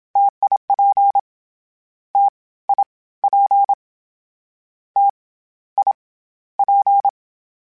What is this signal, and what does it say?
Morse code, heard as a keyed tone, about 27 wpm, 800 Hz. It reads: TIP T I P T I P
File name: Difficult Combos / Difficult Combos-mp3s / T AOEI @25